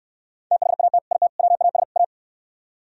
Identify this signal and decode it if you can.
Morse code, heard as a keyed tone, about 67 wpm, 700 Hz. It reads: THAT IT WAS A